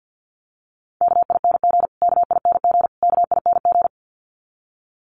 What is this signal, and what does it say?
Morse code, heard as a keyed tone, about 50 wpm, 700 Hz. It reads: KING KING KING
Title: Top 500 Words Repeated 50wpm